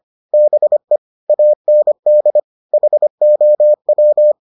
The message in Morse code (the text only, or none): BE AND HOW